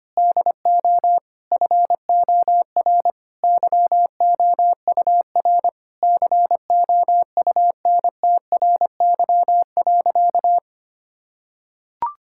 Morse code, heard as a keyed tone, about 25 wpm, 700 Hz. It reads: DO FOR YOUR COUNTRY.
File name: practice / 25wpm / Sentences with 9-Word Phrases - 25wpm